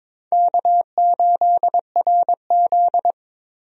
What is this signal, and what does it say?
Morse code, heard as a keyed tone, about 22 wpm, 700 Hz. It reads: K8RZ
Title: USA Call Signs - 4 Characters - Repeated 3x - 22wpm